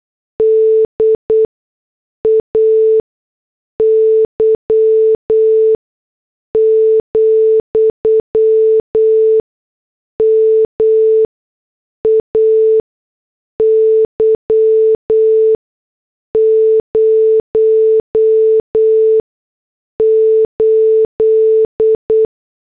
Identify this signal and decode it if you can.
Morse code, heard as a keyed tone, about 8 wpm, spaced as for 6 wpm, 440 Hz. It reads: DAY, MAY 08
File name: morse / headlines